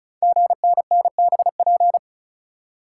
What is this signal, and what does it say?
Morse code, heard as a keyed tone, about 36 wpm, 700 Hz. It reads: GNNBP
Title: Code Groups - 5 Letters - 35wpm